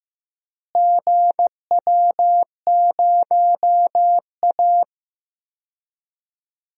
Morse code, heard as a keyed tone, about 15 wpm, 700 Hz. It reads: GW0A
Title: DXCC-100 Call Signs - 4 Characters - Repeated 2x - 15wpm